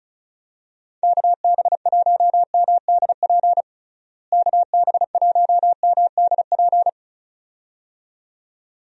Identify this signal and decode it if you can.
Morse code, heard as a keyed tone, about 36 wpm, 700 Hz. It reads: KB1MDP KB1MDP
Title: USA Call Signs - 6 Characters - Repeated 2x - 35wpm